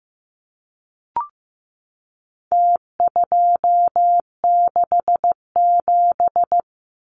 Morse code, heard as a keyed tone, about 15 wpm, 700 Hz. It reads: T267
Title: Numbers - Contest Style - 4-Digit 15wpm